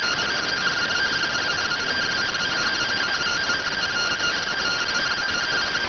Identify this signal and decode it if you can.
Two-tone teleprinter signal, about 170 Hz shift.